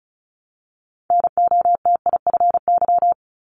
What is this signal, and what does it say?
Morse code, heard as a keyed tone, about 36 wpm, 700 Hz. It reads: NOTIFY
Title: Multi-letter Phonemes 35wpm